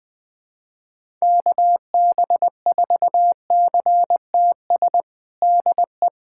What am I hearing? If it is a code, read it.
Morse code, heard as a keyed tone, about 20 wpm, 700 Hz. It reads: KB4CTS DE
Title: USA Callsign DE Callsign 20wpm